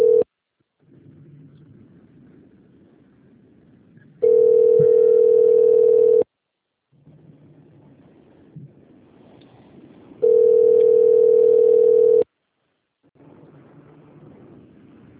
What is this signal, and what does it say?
Ringback tone (North America), 440 + 480 Hz.